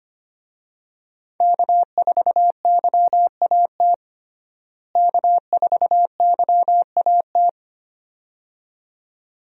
Morse code, heard as a keyed tone, about 25 wpm, 700 Hz. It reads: K4YAT K4YAT
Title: USA Call Signs - 5 Characters - Repeated 2x - 25wpm